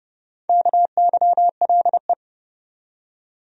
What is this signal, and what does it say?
Morse code, heard as a keyed tone, about 30 wpm, 700 Hz. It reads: KYLE
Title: USA Common Names - 30wpm